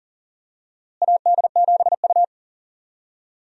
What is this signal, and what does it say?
Morse code, heard as a keyed tone, about 40 wpm, 700 Hz. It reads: AD7U
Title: USA Call Signs - 4 Characters - Repeated 3x - 40wpm